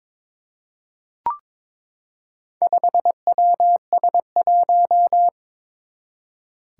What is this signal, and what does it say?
Morse code, heard as a keyed tone, about 23 wpm, 700 Hz. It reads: 5WS1
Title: Four Challenging Letter-Number Combinations 22wpm